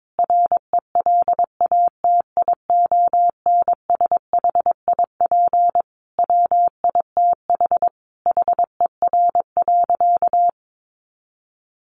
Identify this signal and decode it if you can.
Morse code, heard as a keyed tone, about 22 wpm, 700 Hz. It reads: RELATIONSHIP WITH HER.
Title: Sentences from Top 1100 Words 22wpm